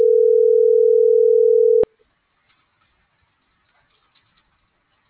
Ringback tone (North America), 440 + 480 Hz.